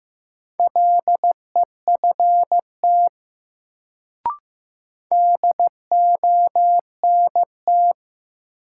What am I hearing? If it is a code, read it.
Morse code, heard as a keyed tone, about 15 wpm, 700 Hz. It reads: LEFT DONT
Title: Sentences from Top 300 Words - New Word Drill - 15wpm